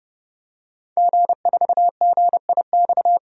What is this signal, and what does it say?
Morse code, heard as a keyed tone, about 32 wpm, 700 Hz. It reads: G4GIX
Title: DXCC-100 Call Signs - 5 Characters - 30wpm